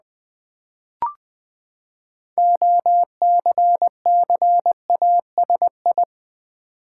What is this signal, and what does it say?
Morse code, heard as a keyed tone, about 20 wpm, 700 Hz. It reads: OCCASI
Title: Build A Word - 6 Letters At A Time - 12-Letter Words - 20wpm